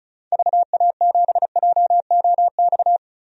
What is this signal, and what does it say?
Morse code, heard as a keyed tone, about 36 wpm, 700 Hz. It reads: VA7JOX
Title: DXCC-100 Call Signs - 6 Characters - 35wpm